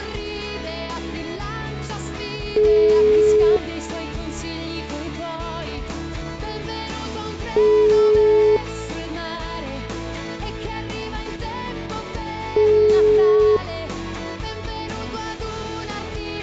Ringback tone (Europe, 425 Hz).